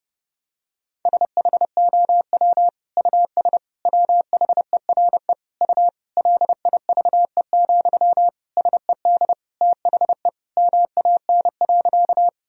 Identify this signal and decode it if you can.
Morse code, heard as a keyed tone, about 30 wpm, 700 Hz. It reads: SHOW US WHERE U LIVE, SED THE MAN.
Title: Sentences from Top 200 Words and Abbreviations 30wpm